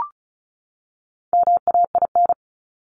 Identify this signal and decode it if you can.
Morse code, heard as a keyed tone, about 34 wpm, 700 Hz. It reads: MAIN
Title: Top 300 Words 35wpm